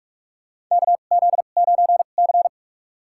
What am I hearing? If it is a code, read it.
Morse code, heard as a keyed tone, about 46 wpm, spaced as for 33 wpm, 700 Hz. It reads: KZ9C
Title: USA Call Signs - 4 Characters - 2x Character Spacing - 45wpm